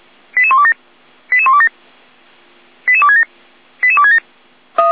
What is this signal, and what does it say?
Five-tone selective call (ZVEI), heard as quick runs of five tones, tones of about 80 ms. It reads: 88127 88127 88267 88267
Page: ZVEI